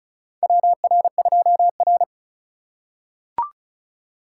Morse code, heard as a keyed tone, about 36 wpm, 700 Hz. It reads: WR2R
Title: USA Call Signs - 4 Characters - 4x Character Spacing - 35wpm